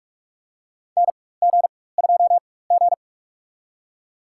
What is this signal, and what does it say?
Morse code, heard as a keyed tone, about 46 wpm, spaced as for 21 wpm, 700 Hz. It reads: NG2G
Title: USA Call Signs - 4 Characters - 4x Character Spacing - 45wpm